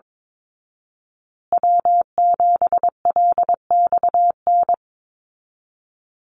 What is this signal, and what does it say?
Morse code, heard as a keyed tone, about 22 wpm, 700 Hz. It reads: W7LXN
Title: Call Signs Repeated 3x - 22wpm